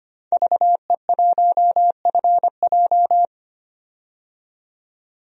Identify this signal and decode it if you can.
Morse code, heard as a keyed tone, about 25 wpm, 700 Hz. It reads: VE1FJ
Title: DXCC-100 Call Signs - 5 Characters - 25wpm